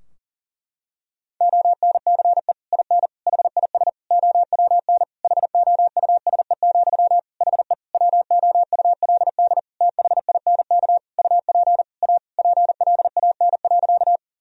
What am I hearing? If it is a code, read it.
Morse code, heard as a keyed tone, about 40 wpm, 700 Hz. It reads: ONCE IN HIS OWN HOUSE, HE WOULD THINK UP A PLAN.
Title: Sentences from Top 1000 Words - Review Part 2 - 40wpm